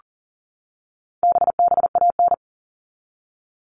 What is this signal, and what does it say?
Morse code, heard as a keyed tone, about 40 wpm, 700 Hz. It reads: BBAN
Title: All Cut Numbers - 4-Digit 40wpm